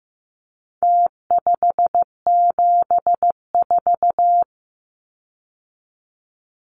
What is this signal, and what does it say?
Morse code, heard as a keyed tone, about 15 wpm, 700 Hz. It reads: T574
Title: Numbers - Contest Style - 4-Digit 15wpm